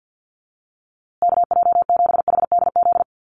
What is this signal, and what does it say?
Morse code, heard as a keyed tone, about 50 wpm, 700 Hz. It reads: KJ7HDZ
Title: USA Call Signs Repeated 3x 50wpm